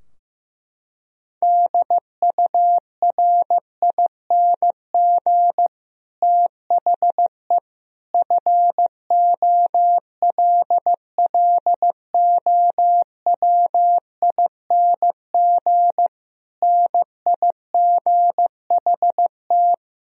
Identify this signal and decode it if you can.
Morse code, heard as a keyed tone, about 15 wpm, 700 Hz. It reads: DURING THE FOLLOWING NIGHT
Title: Sentences from Top 1000 Words - Review Part 4 - 15wpm